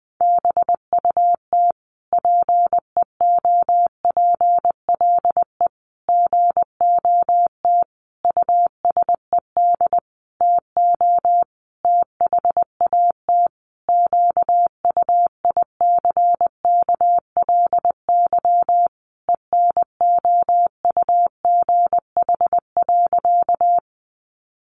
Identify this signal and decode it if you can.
Morse code, heard as a keyed tone, about 20 wpm, 700 Hz. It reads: BUT PEOPLE GOT USED TO THAT QUICKLY ENOUGH.
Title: Sentences from Top 700 Words 20wpm